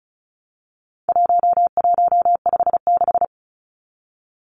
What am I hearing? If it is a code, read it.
Morse code, heard as a keyed tone, about 34 wpm, 700 Hz. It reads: 1156
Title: Numbers - 4-Digit 35wpm